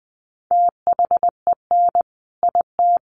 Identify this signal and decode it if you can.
Morse code, heard as a keyed tone, about 20 wpm, 700 Hz. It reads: THEN IT
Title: A Sound Of Thunder 20wpm